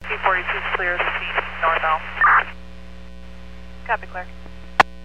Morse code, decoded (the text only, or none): none